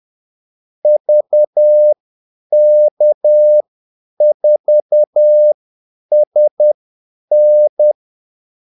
Morse code, heard as a keyed tone, about 10 wpm, spaced as for 8 wpm, 600 Hz. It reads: VK4SN
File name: Callsigns-10-8wpm